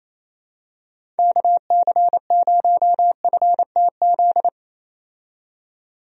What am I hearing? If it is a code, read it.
Morse code, heard as a keyed tone, about 27 wpm, 700 Hz. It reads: KC0FTZ
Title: USA Call Signs - 6 Characters - 28wpm